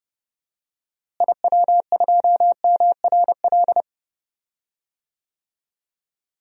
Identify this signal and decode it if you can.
Morse code, heard as a keyed tone, about 30 wpm, 700 Hz. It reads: IW2MRL
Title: DXCC-100 Call Signs - 6 Characters - Repeated 3x - 30wpm